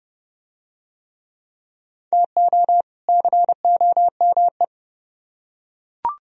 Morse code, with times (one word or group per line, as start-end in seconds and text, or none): 2.1-2.8 TO
3.1-4.6 COME